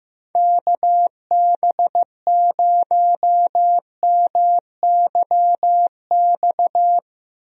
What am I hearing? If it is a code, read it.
Morse code, heard as a keyed tone, about 15 wpm, 700 Hz. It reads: KB0MYX
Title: USA Call Signs - 6 Characters - Repeated 3x - 15wpm